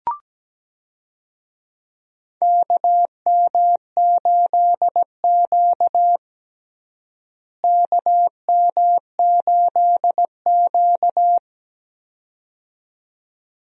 Morse code, heard as a keyed tone, about 17 wpm, 700 Hz. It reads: KM8Q KM8Q
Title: USA Call Signs - 4 Characters - Repeated 2x - 17wpm